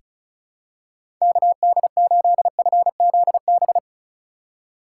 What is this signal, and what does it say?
Morse code, heard as a keyed tone, about 34 wpm, 700 Hz. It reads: KD8FZB
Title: USA Call Signs - 6 Characters - 3x Character Spacing - 35wpm